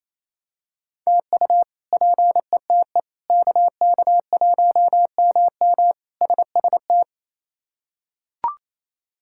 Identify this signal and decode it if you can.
Morse code, heard as a keyed tone, about 28 wpm, 700 Hz. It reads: TU PETE KK1MM SST
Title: K1USN SST QSO Practice - 28wpm